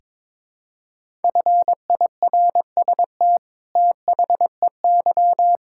Morse code, heard as a keyed tone, about 22 wpm, 700 Hz. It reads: FIRST THEY
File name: practice / 22wpm / Sentences with 8-Word Phrases - 22wpm